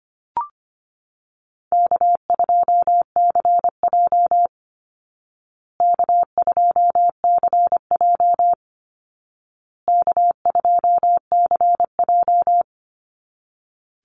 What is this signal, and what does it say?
Morse code, heard as a keyed tone, about 25 wpm, 700 Hz. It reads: K2CJ K2CJ K2CJ
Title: Call Signs Repeated 2x - 25wpm